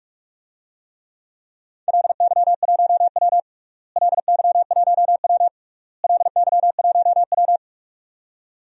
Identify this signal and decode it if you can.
Morse code, heard as a keyed tone, about 46 wpm, 700 Hz. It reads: LY1W LY1W LY1W
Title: DXCC-100 Call Signs - 4 Characters - Repeated 3x - 45wpm